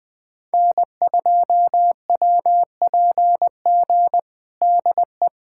Morse code, heard as a keyed tone, about 20 wpm, 700 Hz. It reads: N2WPG DE
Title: USA Callsign DE Callsign 20wpm